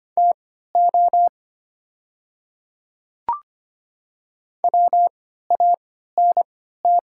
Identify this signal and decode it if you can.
Morse code, heard as a keyed tone, about 25 wpm, spaced as for 14 wpm, 700 Hz. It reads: TO WANT
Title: 3x Character Spacing - Top 200 Words - 25wpm